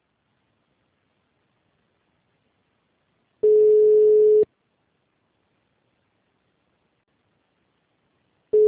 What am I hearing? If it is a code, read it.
Ringback tone (Europe, 425 Hz).
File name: out- -2012- - - .882